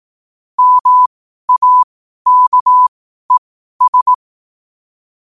Morse code, heard as a keyed tone, about 18 wpm, spaced as for 12 wpm, 1000 Hz. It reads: MAKES
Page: New quotes every day in morse code at 12 Words per minute.